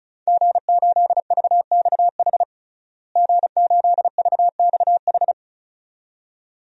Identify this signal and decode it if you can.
Morse code, heard as a keyed tone, about 36 wpm, 700 Hz. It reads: G8VXH G8VXH
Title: DXCC-100 Call Signs - 5 Characters - Repeated 2x - 35wpm